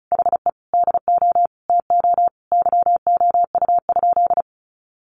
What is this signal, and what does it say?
Morse code, heard as a keyed tone, about 34 wpm, 700 Hz. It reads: HE DO TO YOU?
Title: Sentences from Top 100 Words 35wpm